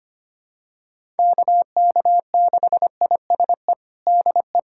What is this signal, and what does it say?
Morse code, heard as a keyed tone, about 25 wpm, 700 Hz. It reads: KK6ISE DE
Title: USA Callsign DE Callsign - 6 Characters - 25wpm